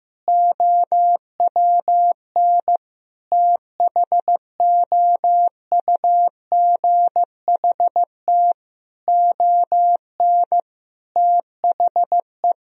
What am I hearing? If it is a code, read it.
Morse code, heard as a keyed tone, about 15 wpm, 700 Hz. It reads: OWN THOUGHT ON THE
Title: Sentences from Top 1000 Words - Review Part 1 - 15wpm